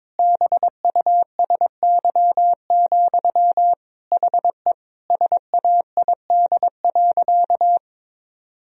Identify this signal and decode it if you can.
Morse code, heard as a keyed tone, about 22 wpm, 700 Hz. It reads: BUSY, HE SAID.
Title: Sentences from Top 1000 Words - Review Part 3 - 22wpm